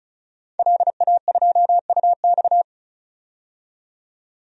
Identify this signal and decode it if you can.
Morse code, heard as a keyed tone, about 36 wpm, 700 Hz. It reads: LA2UX
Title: DXCC-100 Call Signs - 5 Characters - 35wpm